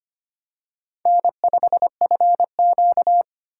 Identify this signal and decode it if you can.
Morse code, heard as a keyed tone, about 25 wpm, 700 Hz. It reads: N5FQ
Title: USA Call Signs - 4 Characters - 3x Character Spacing - 25wpm